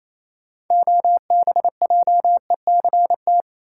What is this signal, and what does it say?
Morse code, heard as a keyed tone, about 28 wpm, 700 Hz. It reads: OBJECT
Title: Sentences from Top 500 Words - New Word Drill - 28wpm